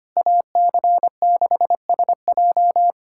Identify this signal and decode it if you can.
Morse code, heard as a keyed tone, about 25 wpm, 700 Hz. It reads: AC6SJ
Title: USA Callsign DE Callsign - 5 Characters - 25wpm